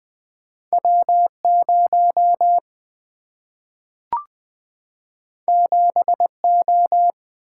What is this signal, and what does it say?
Morse code, heard as a keyed tone, about 20 wpm, 700 Hz. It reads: W 0 7 O
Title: Single-Letter Single-Number 20wpm